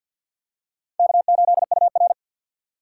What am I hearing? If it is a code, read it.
Morse code, heard as a keyed tone, about 53 wpm, 700 Hz. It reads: K8UR
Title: USA Call Signs - 4 Characters - 3x Character Spacing - 50wpm